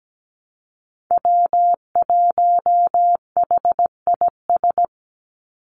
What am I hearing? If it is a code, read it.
Morse code, heard as a keyed tone, about 17 wpm, 700 Hz. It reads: W1HIS
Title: Call Signs - 17wpm